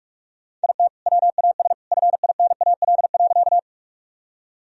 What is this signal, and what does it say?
Morse code, heard as a keyed tone, about 43 wpm, 700 Hz. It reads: IT WAS FINAL.
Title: Sentences from Top 1000 Words - Review Part 2 - 45wpm